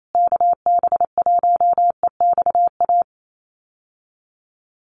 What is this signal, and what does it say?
Morse code, heard as a keyed tone, about 28 wpm, 700 Hz. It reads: KB1EXA
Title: USA Call Signs Repeated 3x 28wpm